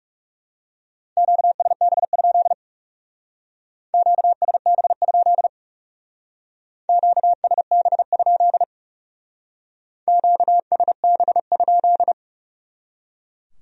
Morse code, heard as a keyed tone, about 40 wpm, 700 Hz. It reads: QSB? QSB? QSB? QSB?